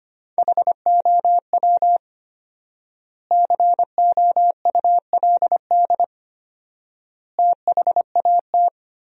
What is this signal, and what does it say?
Morse code, heard as a keyed tone, about 25 wpm, 700 Hz. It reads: HOW COULD THAT
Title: 4x Word-Spacing - Sentences from Top 400 Words - 25wpm